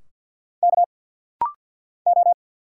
Morse code, heard as a keyed tone, about 50 wpm, 700 Hz. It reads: K O
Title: Single Letters - Rapid-Fire 50wpm